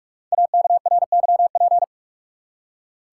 Morse code, heard as a keyed tone, about 44 wpm, 700 Hz. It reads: AKRYP
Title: Code Groups - 5 Letters - 45wpm